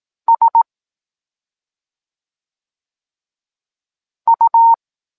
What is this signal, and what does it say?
Morse code, heard as a keyed tone, about 18 wpm, 920 Hz. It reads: S U
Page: Tekst 14 – doskonalenie odbioru poznanych liter